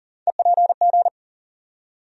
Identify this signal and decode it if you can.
Morse code, heard as a keyed tone, about 40 wpm, 700 Hz. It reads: EPG
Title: USA Call Sign Postfixes - 3 Letters 40wpm